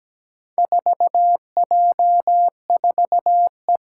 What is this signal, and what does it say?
Morse code, heard as a keyed tone, about 17 wpm, 700 Hz. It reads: 4J4E
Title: Four Challenging Letter-Number Combinations 17wpm